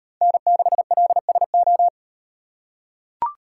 Morse code, heard as a keyed tone, about 36 wpm, 700 Hz. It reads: N6LSO